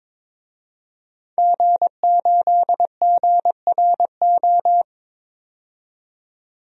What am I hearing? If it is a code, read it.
Morse code, heard as a keyed tone, about 23 wpm, 700 Hz. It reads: G8GRO
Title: DXCC-100 Call Signs - 5 Characters - 22wpm